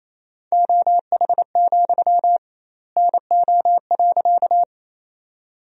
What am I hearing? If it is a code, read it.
Morse code, heard as a keyed tone, about 28 wpm, 700 Hz. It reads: OH, NO.
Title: 2x Word-Spacing - Sentences from Top 500 Words - 28wpm